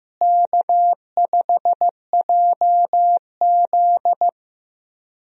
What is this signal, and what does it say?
Morse code, heard as a keyed tone, about 15 wpm, 700 Hz. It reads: K5JZ
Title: USA Call Signs - 4 Characters - Repeated 2x - 15wpm